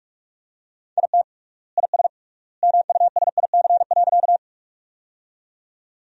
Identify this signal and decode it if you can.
Morse code, heard as a keyed tone, about 46 wpm, 700 Hz. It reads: IT IS MUSIC.
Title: 3x Word-Spacing - Sentences from Top 300 Words - 45wpm